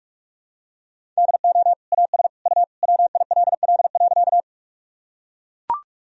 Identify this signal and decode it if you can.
Morse code, heard as a keyed tone, about 43 wpm, 700 Hz. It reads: DO AS U WILL.
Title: Sentences from Top 100 Words and Abbreviations 45wpm